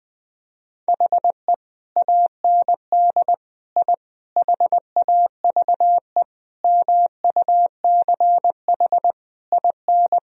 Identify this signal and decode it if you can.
Morse code, heard as a keyed tone, about 20 wpm, 700 Hz. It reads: HE AND I HAVE MUCH IN
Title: Sentences from Top 1000 Words - Review Part 3 - 20wpm